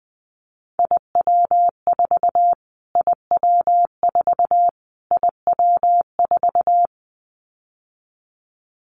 Morse code, heard as a keyed tone, about 20 wpm, 700 Hz. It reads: IW4 IW4 IW4
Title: Callsign Prefixes for Top 25 DXCC - Repeated 3x - 20wpm